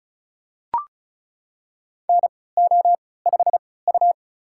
Morse code, heard as a keyed tone, about 34 wpm, spaced as for 20 wpm, 700 Hz. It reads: NO5U
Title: USA Call Signs - 4 Characters - 3x Character Spacing - 35wpm